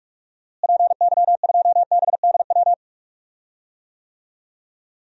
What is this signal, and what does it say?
Morse code, heard as a keyed tone, about 46 wpm, 700 Hz. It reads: PY2BDW
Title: DXCC-100 Call Signs - 6 Characters - Repeated 3x - 45wpm